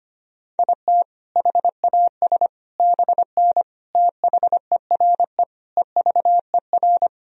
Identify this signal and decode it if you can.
Morse code, heard as a keyed tone, about 25 wpm, 700 Hz. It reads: IT HAS BN THERE EVER
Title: Sentences from Top 400 Words and Abbreviations 25wpm